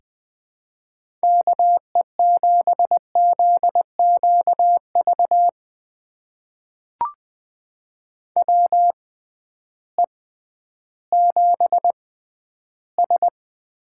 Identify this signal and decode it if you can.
Morse code, heard as a keyed tone, about 20 wpm, spaced as for 12 wpm, 700 Hz. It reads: KE7ZQV WE7S
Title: USA Call Signs - 6x Character Spacing - 20wpm